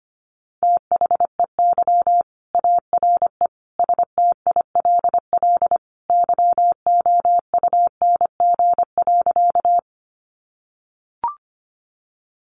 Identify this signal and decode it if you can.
Morse code, heard as a keyed tone, about 25 wpm, 700 Hz. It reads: THEY ARE STILL YOUNG.
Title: Sentences from Top 400 Words 25wpm